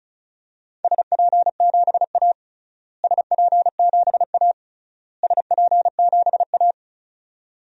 Morse code, heard as a keyed tone, about 34 wpm, 700 Hz. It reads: SP7A SP7A SP7A
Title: DXCC-100 Call Signs - 4 Characters - Repeated 3x - 35wpm